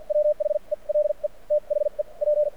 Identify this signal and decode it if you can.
Morse code, heard as a keyed tone, about 50 wpm, spaced as for 40 wpm, 600 Hz. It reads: WHE LE THE P
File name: archived / CRSnet